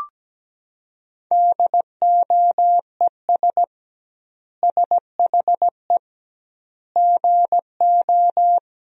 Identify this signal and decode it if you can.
Morse code, heard as a keyed tone, about 17 wpm, 700 Hz. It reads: DOES SHE GO
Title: 2x Word-Spacing - Sentences from Top 400 Words - 17wpm